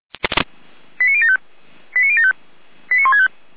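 Five-tone selective call (ZVEI), heard as quick runs of five tones, tones of about 70 ms. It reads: 89075 89075 89165
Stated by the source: ZVEI five-tone selective call received by SDR, feuerwehr-orlamuende.org fire brigade alerting archive.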